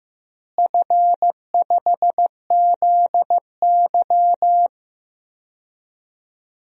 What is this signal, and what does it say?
Morse code, heard as a keyed tone, about 15 wpm, 700 Hz. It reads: F5ZY
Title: DXCC-100 Call Signs - 4 Characters - Repeated 2x - 15wpm